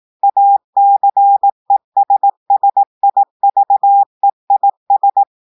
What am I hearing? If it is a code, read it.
Morse code, heard as a keyed tone, about 18 wpm, 800 Hz. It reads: ACESSIVEIS
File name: audios / portugais / portugais texte en clair 1 vitesse 18 mots minutes